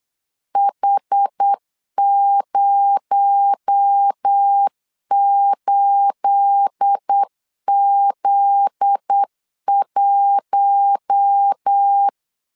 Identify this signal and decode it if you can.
Morse code, heard as a keyed tone, about 8 wpm, 800 Hz. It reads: H08Z1
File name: CW / DET / 1 9 1500C S B 8wpm 5624